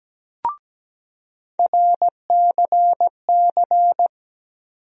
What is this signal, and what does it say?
Morse code, heard as a keyed tone, about 17 wpm, 700 Hz. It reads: RCC
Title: USA Call Sign Postfixes - 3 Letters 17wpm